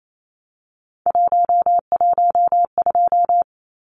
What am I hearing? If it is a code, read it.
Morse code, heard as a keyed tone, about 27 wpm, 700 Hz. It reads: 112
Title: Numbers - 3-Digit 28wpm